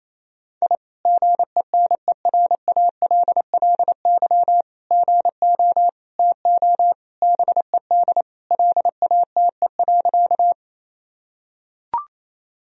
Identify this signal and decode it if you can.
Morse code, heard as a keyed tone, about 28 wpm, 700 Hz. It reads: I GENERALLY GO TO BED LATE.
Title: Sentences from Top 1000 Words - Review Part 4 - 28wpm